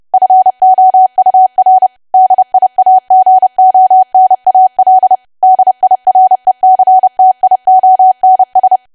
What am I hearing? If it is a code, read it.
Morse code, heard as a keyed tone, about 30 wpm, 750 Hz. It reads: FOUR DIAGONAL DIRECTIONS
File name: cwa / W1AW-Level3 / 30WPM